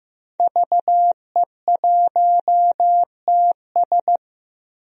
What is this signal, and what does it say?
Morse code, heard as a keyed tone, about 16 wpm, 700 Hz. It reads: VE1TS
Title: DXCC-100 Call Signs - 5 Characters - Repeated 3x - 15wpm